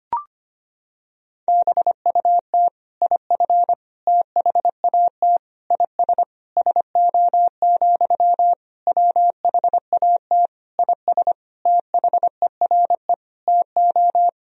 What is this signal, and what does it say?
Morse code, heard as a keyed tone, about 25 wpm, 700 Hz. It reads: BUT IF THAT IS SO, WHAT IS THERE TO
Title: Sentences from Top 200 Words and Abbreviations 25wpm